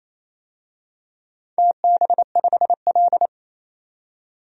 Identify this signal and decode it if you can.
Morse code, heard as a keyed tone, about 27 wpm, 700 Hz. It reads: TB5L
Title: DXCC-100 Call Signs - 4 Characters - 28wpm